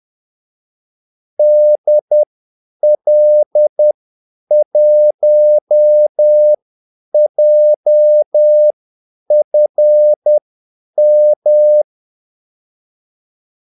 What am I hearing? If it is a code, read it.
Morse code, heard as a keyed tone, about 10 wpm, spaced as for 8 wpm, 600 Hz. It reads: DL1JFM
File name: Callsigns-10-8wpm